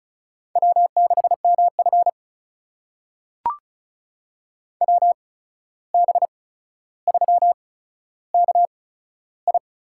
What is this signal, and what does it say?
Morse code, heard as a keyed tone, about 36 wpm, spaced as for 14 wpm, 700 Hz. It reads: W6MF WB3KI
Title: USA Call Signs - 8x Character Spacing - 35wpm